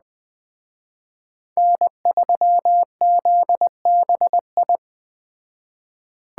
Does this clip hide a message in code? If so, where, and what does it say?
Yes: Morse code from 1.6 to 4.8 s, heard as a keyed tone, about 21 wpm, 700 Hz. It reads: N3ZBI